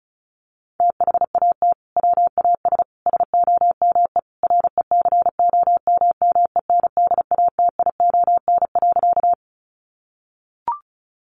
Morse code, heard as a keyed tone, about 34 wpm, 700 Hz. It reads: THAT WAS SOME RECOMMENDATION.
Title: Sentences from Top 1400 Words 35wpm